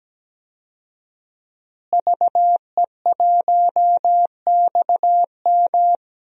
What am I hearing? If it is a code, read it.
Morse code, heard as a keyed tone, about 17 wpm, 700 Hz. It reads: VE1XM
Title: DXCC-100 Call Signs - 5 Characters - Repeated 2x - 17wpm